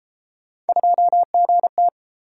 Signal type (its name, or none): Morse code